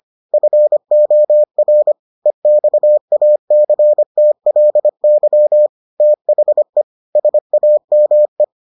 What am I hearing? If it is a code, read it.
Morse code, heard as a keyed tone, about 25 wpm, 600 Hz. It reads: FOR EXACTLY THE SAME